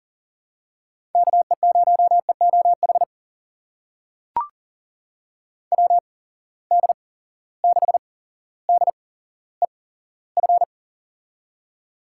Morse code, heard as a keyed tone, about 40 wpm, spaced as for 17 wpm, 700 Hz. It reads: KE0EOH WD6DEF
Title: USA Call Signs - 6 Characters - 8x Character Spacing - 40wpm